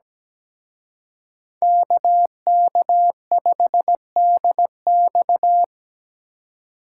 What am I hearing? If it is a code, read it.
Morse code, heard as a keyed tone, about 20 wpm, 700 Hz. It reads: KK5DX
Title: Speed Racing - USA 2x2 Call Signs 17wpm